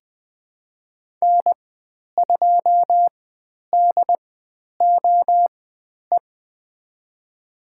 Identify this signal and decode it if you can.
Morse code, heard as a keyed tone, about 20 wpm, spaced as for 10 wpm, 700 Hz. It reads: N2DOE
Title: 095 Lesson 16g - Letter c - Call Signs 20wpm